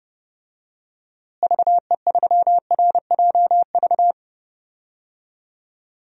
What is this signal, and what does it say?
Morse code, heard as a keyed tone, about 32 wpm, 700 Hz. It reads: VE3RJV
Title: DXCC-100 Call Signs - 6 Characters - Repeated 2x - 30wpm